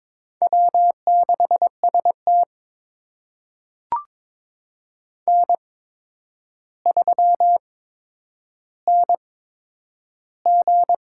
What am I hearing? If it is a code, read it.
Morse code, heard as a keyed tone, about 22 wpm, spaced as for 9 wpm, 700 Hz. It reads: W6ST N3NG
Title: USA Call Signs - 4 Characters - 8x Character Spacing - 22wpm